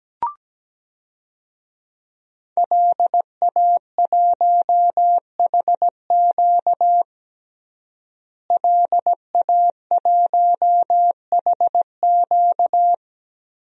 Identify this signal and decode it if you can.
Morse code, heard as a keyed tone, about 17 wpm, 700 Hz. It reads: LA1HQ LA1HQ
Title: DXCC-100 Call Signs - 5 Characters - Repeated 3x - 17wpm